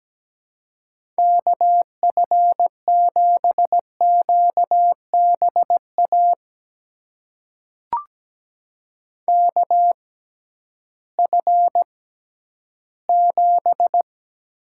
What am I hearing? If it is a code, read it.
Morse code, heard as a keyed tone, about 17 wpm, spaced as for 11 wpm, 700 Hz. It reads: KF7QBA KF7
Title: USA Call Signs - 6 Characters - 6x Character Spacing - 17wpm